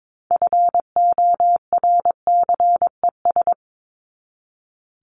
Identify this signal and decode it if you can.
Morse code, heard as a keyed tone, about 22 wpm, 700 Hz. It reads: FORCES
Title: ES:S Suffix - Top 100 Words with Base Words 22wpm